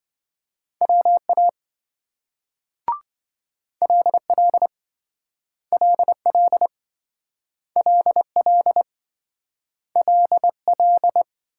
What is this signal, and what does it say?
Morse code, heard as a keyed tone, about 26 wpm, 700 Hz. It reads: WA LL LL LL LL